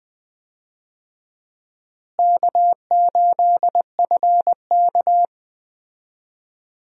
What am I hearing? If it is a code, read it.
Morse code, heard as a keyed tone, about 20 wpm, 700 Hz. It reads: K8FK
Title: USA Call Signs - 4 Characters - Repeated 2x - 20wpm